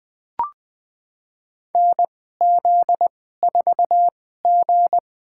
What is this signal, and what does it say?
Morse code, heard as a keyed tone, about 21 wpm, spaced as for 15 wpm, 700 Hz. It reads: NZ4G
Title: USA Call Signs - 4 Characters - 2x Character Spacing - 20wpm